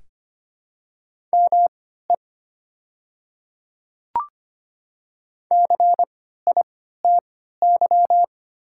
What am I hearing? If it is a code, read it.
Morse code, heard as a keyed tone, about 25 wpm, spaced as for 14 wpm, 700 Hz. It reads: ME CITY
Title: 3x Character Spacing - Top 300 Words - 25wpm